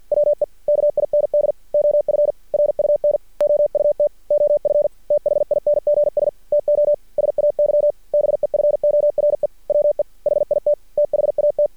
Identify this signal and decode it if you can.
Morse code, heard as a keyed tone, about 50 wpm, 590 Hz. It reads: WE KIND OF RUN OUT OF THINGS TO SAY BEFORE WE HIT THAT
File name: archived / CRSnet